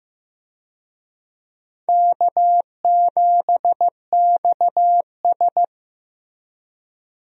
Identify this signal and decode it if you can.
Morse code, heard as a keyed tone, about 15 wpm, 700 Hz. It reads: K7XS
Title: USA Call Signs - 4 Characters - Repeated 2x - 15wpm